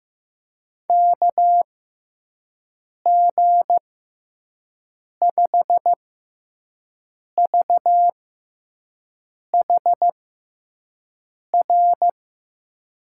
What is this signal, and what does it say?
Morse code, heard as a keyed tone, about 15 wpm, spaced as for 5 wpm, 700 Hz. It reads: KG5VHR
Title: USA Call Signs - 6 Characters - 6x Character Spacing - 15wpm